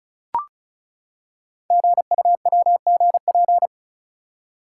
Morse code, heard as a keyed tone, about 36 wpm, 700 Hz. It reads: GUWGP
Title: Code Groups - 5 Letters - 35wpm